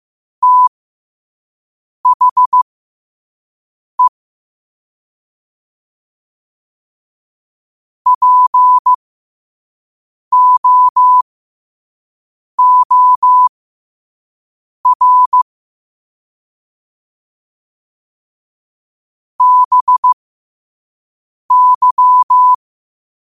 Morse code, heard as a keyed tone, about 15 wpm, spaced as for 5 wpm, 1000 Hz. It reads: THE POOR BY
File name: qotdfiles / 5wpm